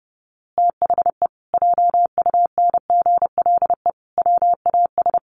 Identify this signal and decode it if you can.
Morse code, heard as a keyed tone, about 30 wpm, 700 Hz. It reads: THE JUNGLE WAS
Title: A Sound Of Thunder 30wpm